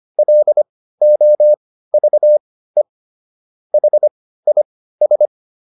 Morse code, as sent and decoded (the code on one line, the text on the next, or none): .-.. --- ...- . / .... .. ...
LOVE HIS